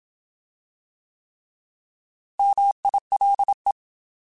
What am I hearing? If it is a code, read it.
Morse code, heard as a keyed tone, about 27 wpm, 800 Hz. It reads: MILE